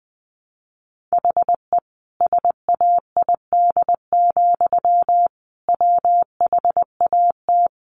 Morse code, heard as a keyed tone, about 20 wpm, 700 Hz. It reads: HE SAID, WHAT
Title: Sentences from Top 700 Words 20wpm